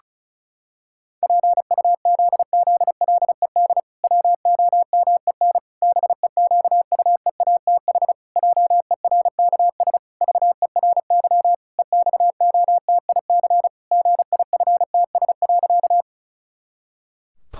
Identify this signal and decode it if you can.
Morse code, heard as a keyed tone, about 34 wpm, 700 Hz. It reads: PUZZLED WOMEN BEQUEATH JERKS VERY EXOTIC GIFTS.
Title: Pangram Challenge v2 - 35wpm